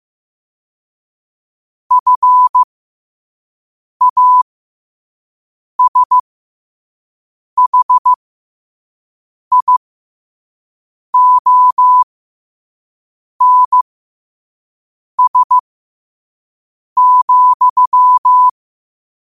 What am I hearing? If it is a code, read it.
Morse code, heard as a keyed tone, about 15 wpm, spaced as for 5 wpm, 1000 Hz. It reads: FASHIONS,
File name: qotdfiles / 5wpm